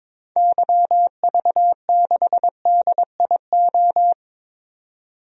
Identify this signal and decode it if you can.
Morse code, heard as a keyed tone, about 22 wpm, 700 Hz. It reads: YV6DIO
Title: DXCC-100 Call Signs - 6 Characters - 22wpm